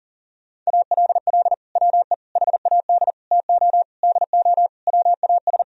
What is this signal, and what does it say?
Morse code, heard as a keyed tone, about 37 wpm, 700 Hz. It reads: ALL WE HAD TO DO WAS
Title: Sentences from Top 1000 Words - Review Part 4 - 40wpm